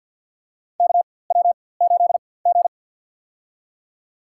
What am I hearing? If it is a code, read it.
Morse code, heard as a keyed tone, about 53 wpm, spaced as for 24 wpm, 700 Hz. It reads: KW8G
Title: USA Call Signs - 4x Character Spacing - 50wpm